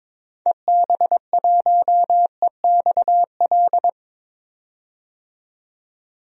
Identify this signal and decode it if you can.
Morse code, heard as a keyed tone, about 22 wpm, 700 Hz. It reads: EB1EXL
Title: DXCC-100 Call Signs - 6 Characters - 22wpm